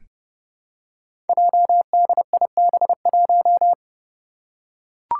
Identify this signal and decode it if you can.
Morse code, heard as a keyed tone, about 28 wpm, 700 Hz. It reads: JDIB1
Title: Five Challenging Letter-Number Combinations 30wpm